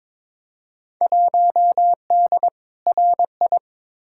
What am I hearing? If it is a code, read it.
Morse code, heard as a keyed tone, about 22 wpm, 700 Hz. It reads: 1D RI
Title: Field Day Exchanges - 22wpm